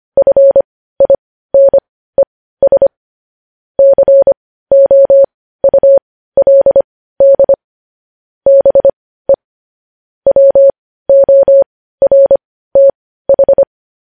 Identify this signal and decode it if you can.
Morse code, heard as a keyed tone, about 25 wpm, spaced as for 15 wpm, 550 Hz. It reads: FINES COULD BE WORTH
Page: News Headlines in Morse Code at 15 WPM